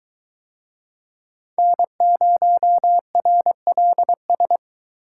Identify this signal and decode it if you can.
Morse code, heard as a keyed tone, about 22 wpm, 700 Hz. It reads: N0RLS